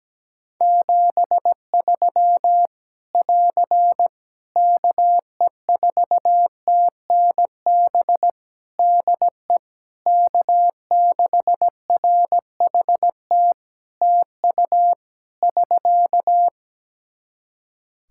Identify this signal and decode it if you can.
Morse code, heard as a keyed tone, about 17 wpm, 700 Hz. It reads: 73 <AR> KE4TNB DE K6RHT TU <SK>
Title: Practice QSO - Exchange 4 - Ending - 17wpm